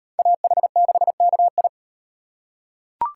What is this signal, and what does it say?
Morse code, heard as a keyed tone, about 37 wpm, 700 Hz. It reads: AH6KI